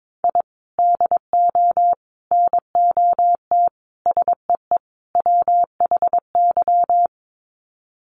Morse code, heard as a keyed tone, about 22 wpm, 700 Hz. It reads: I DO NOT SEE WHY
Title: Sets of 5 Words Encore 22wpm